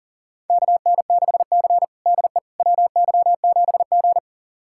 Morse code, heard as a keyed tone, about 40 wpm, 700 Hz. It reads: KN6C DE WY7G
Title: USA Callsign DE Callsign - 4 Characters - 40wpm